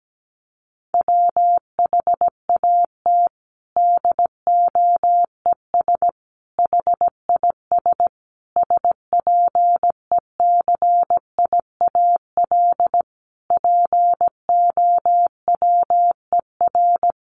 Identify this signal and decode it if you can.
Morse code, heard as a keyed tone, about 17 wpm, 700 Hz. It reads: WHAT DOES HIS SPECIAL POWER
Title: Sentences from Top 1400 Words 17wpm